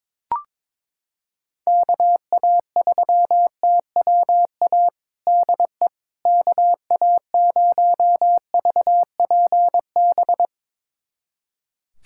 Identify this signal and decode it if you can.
Morse code, heard as a keyed tone, about 22 wpm, 700 Hz. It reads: KA3TWA DE KA0VPB
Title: USA Callsign DE Callsign - 6 Characters - 22wpm